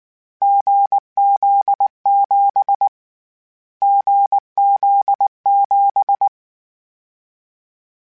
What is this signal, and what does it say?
Morse code, heard as a keyed tone, about 19 wpm, 800 Hz. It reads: GZ7 GZ7
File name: Difficult Combos / Difficult Combos-mp3s / awj1 ndb6 gz7 @18